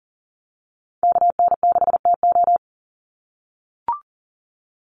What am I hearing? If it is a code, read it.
Morse code, heard as a keyed tone, about 40 wpm, 700 Hz. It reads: KN6TO
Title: USA Call Signs 40wpm